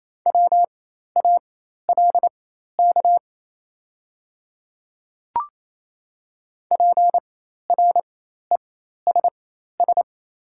Morse code, heard as a keyed tone, about 27 wpm, spaced as for 13 wpm, 700 Hz. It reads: WALK PRESS
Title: 4x Character Spacing - Top 300 Words - 28wpm